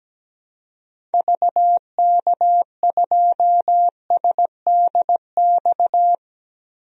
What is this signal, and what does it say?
Morse code, heard as a keyed tone, about 17 wpm, 700 Hz. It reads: VK2SDX
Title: DXCC-100 Call Signs - 6 Characters - Repeated 3x - 17wpm